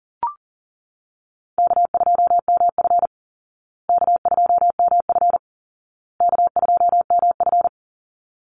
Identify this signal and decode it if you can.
Morse code, heard as a keyed tone, about 37 wpm, 700 Hz. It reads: K2MF K2MF K2MF
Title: Easy Call Signs Repeated 3x - 40wpm